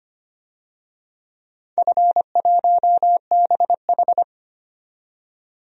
Morse code, heard as a keyed tone, about 25 wpm, 700 Hz. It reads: F1BH
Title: DXCC-100 Call Signs - 4 Characters - 25wpm